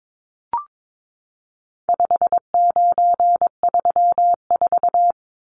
Morse code, heard as a keyed tone, about 22 wpm, 700 Hz. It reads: 5934
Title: Numbers - 4-Digit 22wpm